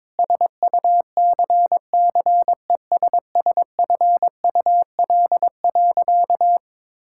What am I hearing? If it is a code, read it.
Morse code, heard as a keyed tone, about 22 wpm, 700 Hz. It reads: SUCCESSFUL.
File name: Sentences with 7-Word Phrases - 22wpm